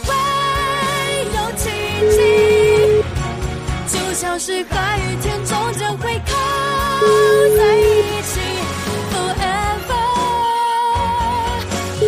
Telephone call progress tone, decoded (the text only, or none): ringback tone